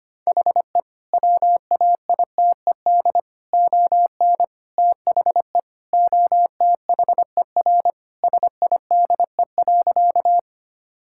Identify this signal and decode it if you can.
Morse code, heard as a keyed tone, about 25 wpm, 700 Hz. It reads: HE WAITED ON THE OTHER SIDE.
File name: practice / 25wpm / Sentences with 4-Word Phrases - 25wpm